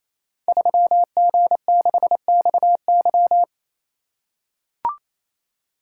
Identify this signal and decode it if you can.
Morse code, heard as a keyed tone, about 27 wpm, 700 Hz. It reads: 3G6XY
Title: Code Groups - 5 Letter:Numbers - 28wpm